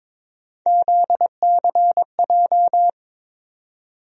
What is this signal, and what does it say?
Morse code, heard as a keyed tone, about 21 wpm, 700 Hz. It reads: ZCJ
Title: USA Call Sign Postfixes - 3 Letters 22wpm